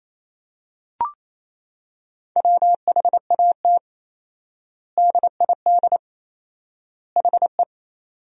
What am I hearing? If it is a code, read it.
Morse code, heard as a keyed tone, about 28 wpm, 700 Hz. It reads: WHAT DID HE
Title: 4x Word-Spacing - Sentences from Top 300 Words - 28wpm